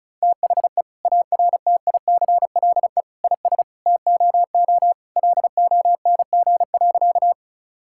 Morse code, heard as a keyed tone, about 36 wpm, 700 Hz. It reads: THE ARTICLE IS TOO LONG.
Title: Sentences from Top 1000 Words - Review Part 5 - 35wpm